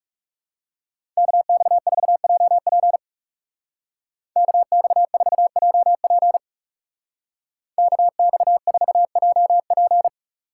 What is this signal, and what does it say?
Morse code, heard as a keyed tone, about 37 wpm, 690 Hz. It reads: KX4JP KX4JP KX4JP
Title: Speed Racing - USA Call Signs 30wpm